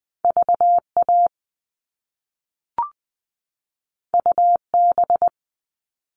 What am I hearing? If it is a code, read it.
Morse code, heard as a keyed tone, about 20 wpm, 700 Hz. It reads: VA UB
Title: All Cut Numbers - 2-Digit 20wpm